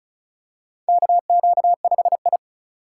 Morse code, heard as a keyed tone, about 36 wpm, 700 Hz. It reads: KQ5I
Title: USA Call Signs - 4 Characters - 4x Character Spacing - 35wpm